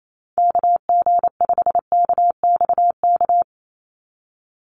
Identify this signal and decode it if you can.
Morse code, heard as a keyed tone, about 28 wpm, 700 Hz. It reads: KG5KXK
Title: USA Call Signs 28wpm